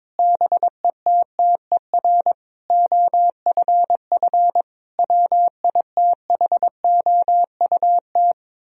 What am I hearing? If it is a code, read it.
Morse code, heard as a keyed tone, about 22 wpm, 700 Hz. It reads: BETTER OFF WITHOUT
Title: Sentences from Top 1000 Words - Review Part 3 - 22wpm